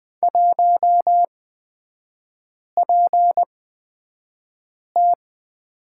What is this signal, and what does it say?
Morse code, heard as a keyed tone, about 20 wpm, spaced as for 5 wpm, 700 Hz. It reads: 1PT
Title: 224 Lesson 36e - BT - Three Characters 20wpm